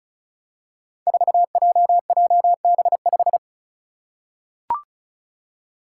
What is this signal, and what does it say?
Morse code, heard as a keyed tone, about 36 wpm, 700 Hz. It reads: 4JJB5
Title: Code Groups - 5 Letter:Numbers - 35wpm